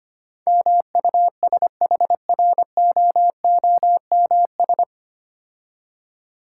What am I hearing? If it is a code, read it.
Morse code, heard as a keyed tone, about 25 wpm, 700 Hz. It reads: MUSHROOMS
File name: practice / 25wpm / Difficult Words with Hints - 25wpm